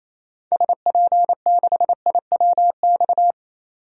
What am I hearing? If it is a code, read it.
Morse code, heard as a keyed tone, about 27 wpm, 700 Hz. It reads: SP6IWX
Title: DXCC-100 Call Signs - 6 Characters - 28wpm